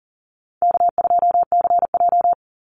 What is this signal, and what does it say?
Morse code, heard as a keyed tone, about 40 wpm, 700 Hz. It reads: K2CJ
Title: Call Signs Repeated 3x - 40wpm